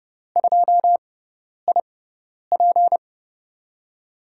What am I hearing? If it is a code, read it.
Morse code, heard as a keyed tone, about 30 wpm, spaced as for 10 wpm, 700 Hz. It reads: 2IP
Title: USA Call Signs - 5 Characters - 6x Character Spacing - 30wpm